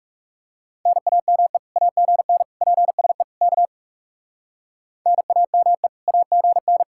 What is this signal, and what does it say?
Morse code, heard as a keyed tone, about 38 wpm, 700 Hz. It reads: NAME AGN PSE K NAME AGN